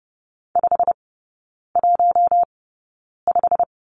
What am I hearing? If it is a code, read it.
Morse code, heard as a keyed tone, about 30 wpm, spaced as for 9 wpm, 690 Hz. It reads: 515
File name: Lesson 16e - Letter c - Three Characters 30wpm